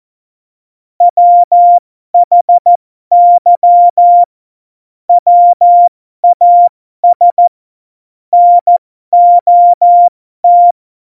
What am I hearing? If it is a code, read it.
Morse code, heard as a keyed tone, about 14 wpm, spaced as for 12 wpm, 700 Hz. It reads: WHY WAS NOT
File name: Morsekurs / Witze